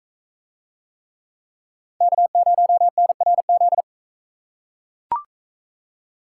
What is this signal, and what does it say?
Morse code, heard as a keyed tone, about 42 wpm, 700 Hz. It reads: K0NRZ